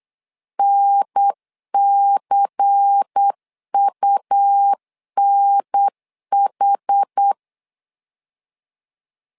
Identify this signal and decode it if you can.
Morse code, heard as a keyed tone, about 8 wpm, 800 Hz. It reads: NCUNH
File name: LEZIONI / 5 LBVC / 5 LBVC 8wpm 2935